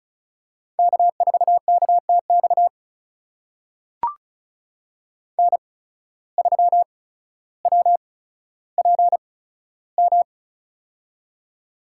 Morse code, heard as a keyed tone, about 36 wpm, spaced as for 15 wpm, 700 Hz. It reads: K4KTX N3WPM
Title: USA Call Signs - 5 Characters - 8x Character Spacing - 35wpm